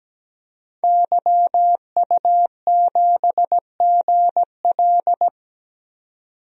Morse code, heard as a keyed tone, about 17 wpm, 700 Hz. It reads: YU7GL
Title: DXCC-100 Call Signs - 5 Characters - 17wpm